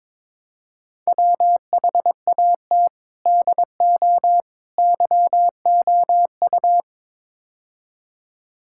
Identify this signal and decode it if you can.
Morse code, heard as a keyed tone, about 22 wpm, 700 Hz. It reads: WHAT DO YOU
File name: practice / 22wpm / Build A Phrase - 5-Word Phrases - 22wpm